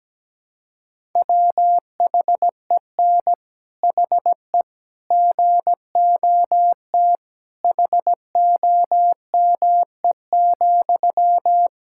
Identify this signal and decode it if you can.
Morse code, heard as a keyed tone, about 17 wpm, 700 Hz. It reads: WHEN HE GOT HOME,
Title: Sentences from Top 1000 Words - Review Part 6 - 17wpm